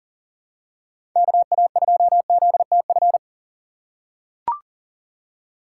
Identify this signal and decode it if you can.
Morse code, heard as a keyed tone, about 40 wpm, 700 Hz. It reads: KA2ZTF
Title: USA Call Signs - 6 Characters - 3x Character Spacing - 40wpm